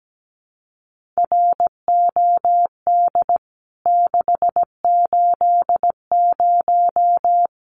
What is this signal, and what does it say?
Morse code, heard as a keyed tone, about 17 wpm, 700 Hz. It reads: ROD 680
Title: CWT Exchange 17wpm